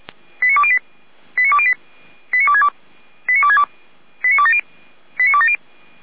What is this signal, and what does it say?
Five-tone selective call (ZVEI), heard as quick runs of five tones, tones of about 70 ms. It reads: 89208 89208 89262 89262 89270 89270